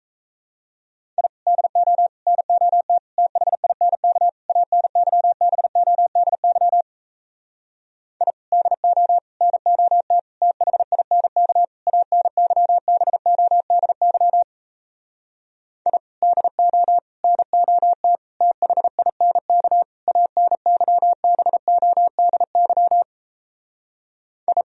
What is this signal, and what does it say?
Morse code, heard as a keyed tone, about 38 wpm, 700 Hz. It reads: I DO NOT THINK ANYBODY I DO NOT THINK ANYBODY I DO NOT THINK ANYBODY I